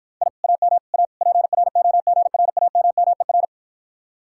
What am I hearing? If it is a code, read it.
Morse code, heard as a keyed tone, about 52 wpm, 700 Hz. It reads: I AM A PROGRAMMER